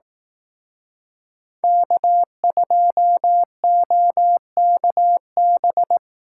Morse code, heard as a keyed tone, about 18 wpm, 700 Hz. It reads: K2OKB
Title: Speed Racing - USA 1x3 Call Signs 15wpm